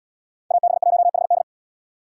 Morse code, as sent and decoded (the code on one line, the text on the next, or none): .- -... .---- .... -..
AB1HD